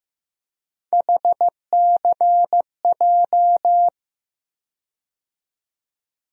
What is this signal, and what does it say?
Morse code, heard as a keyed tone, about 15 wpm, 700 Hz. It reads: HCJ
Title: USA Call Sign Postfixes - 3 Letters 15wpm